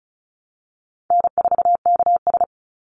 Morse code, heard as a keyed tone, about 34 wpm, 700 Hz. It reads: N4KS
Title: Easy Call Signs Repeated 3x - 35wpm